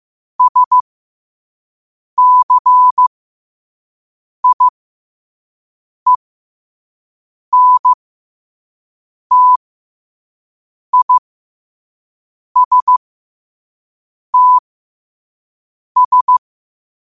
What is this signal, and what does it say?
Morse code, heard as a keyed tone, about 15 wpm, spaced as for 5 wpm, 1000 Hz. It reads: SCIENTISTS
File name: qotdfiles / 5wpm